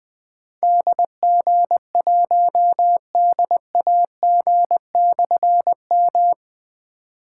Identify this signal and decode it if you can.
Morse code, heard as a keyed tone, about 20 wpm, 700 Hz. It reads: DG1DAG/M
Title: Difficult Call Signs 20wpm